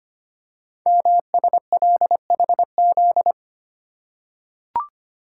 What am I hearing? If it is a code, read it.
Morse code, heard as a keyed tone, about 25 wpm, 700 Hz. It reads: MSLHZ
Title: Code Groups - 5 Letter:Numbers - 25wpm